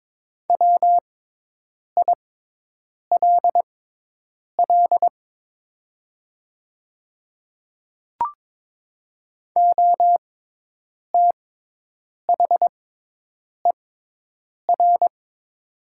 Morse code, heard as a keyed tone, about 22 wpm, spaced as for 8 wpm, 700 Hz. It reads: WILL OTHER
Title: 6x Character Spacing - Top 100 Words - 22wpm